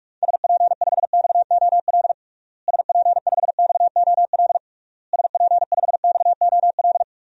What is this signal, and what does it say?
Morse code, heard as a keyed tone, about 41 wpm, 700 Hz. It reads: SP5XOL SP5XOL SP5XOL
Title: DXCC-100 Call Signs - 6 Characters - Repeated 3x - 45wpm